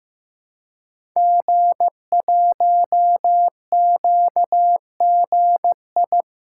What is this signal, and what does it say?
Morse code, heard as a keyed tone, about 15 wpm, 700 Hz. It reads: G1QGI
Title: Code Groups - 5 Letter:Numbers - 15wpm